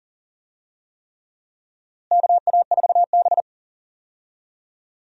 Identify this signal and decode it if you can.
Morse code, heard as a keyed tone, about 40 wpm, 700 Hz. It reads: KA4B
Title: USA Call Signs - 4 Characters - Repeated 3x - 40wpm